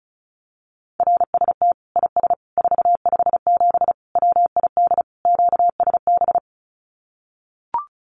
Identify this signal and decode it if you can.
Morse code, heard as a keyed tone, about 34 wpm, 700 Hz. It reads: RST IS 457 WID QSB
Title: Common QSO Phrases 35wpm